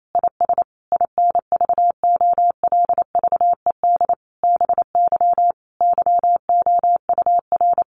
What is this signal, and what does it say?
Morse code, heard as a keyed tone, about 28 wpm, 700 Hz. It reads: IS INVOLVED BY YOUR
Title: Sentences from Top 800 Words 28wpm